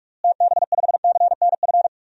Morse code, heard as a keyed tone, about 46 wpm, 700 Hz. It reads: TB5CNF
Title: DXCC-100 Call Signs - 6 Characters - Repeated 2x - 45wpm